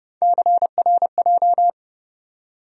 Morse code, heard as a keyed tone, about 30 wpm, 700 Hz. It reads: CRJ
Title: USA Call Sign Postfixes - 3 Letters 30wpm